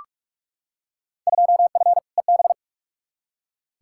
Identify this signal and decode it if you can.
Morse code, heard as a keyed tone, about 42 wpm, 700 Hz. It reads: 2F EB
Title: Field Day Exchanges Encore - 45wpm